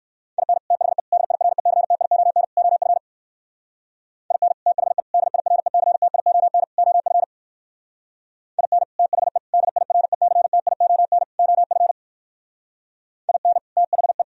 Morse code, heard as a keyed tone, about 55 wpm, 700 Hz. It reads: IN THE DIRECTION OF IN THE DIRECTION OF IN THE DIRECTION OF IN THE
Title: Speed-Racing - Sets of 4 Words 45wpm